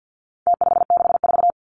Morse code, heard as a keyed tone, about 48 wpm, 700 Hz. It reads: T564
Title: Numbers - Contest Style - 4-Digit 50wpm